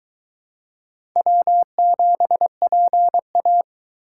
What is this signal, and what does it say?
Morse code, heard as a keyed tone, about 22 wpm, 700 Hz. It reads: W7PA